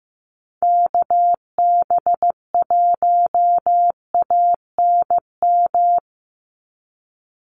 Morse code, heard as a keyed tone, about 15 wpm, 700 Hz. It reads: KB1ANM
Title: USA Call Signs 15wpm